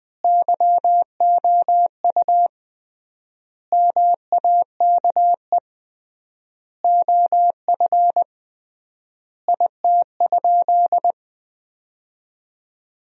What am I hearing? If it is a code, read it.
Morse code, heard as a keyed tone, about 20 wpm, 700 Hz. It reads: YOU MAKE OF IT?
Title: 3x Word-Spacing - Sentences from Top 100 Words - 20wpm